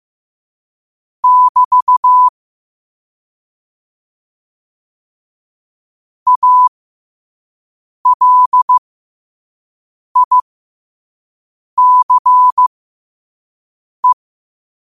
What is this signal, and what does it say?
Morse code, heard as a keyed tone, about 15 wpm, spaced as for 5 wpm, 1000 Hz. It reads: <BT> ALICE
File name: qotdfiles / 5wpm